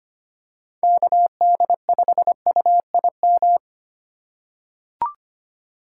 Morse code, heard as a keyed tone, about 25 wpm, 700 Hz. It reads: KD5UIM
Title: USA Call Signs - 3x Character Spacing - 25wpm